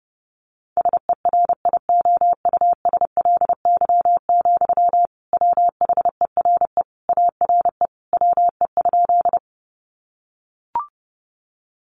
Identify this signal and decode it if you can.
Morse code, heard as a keyed tone, about 30 wpm, 700 Hz. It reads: SERIOUSLY, WHERE ARE WE?
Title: Sentences from Top 1200 Words 30wpm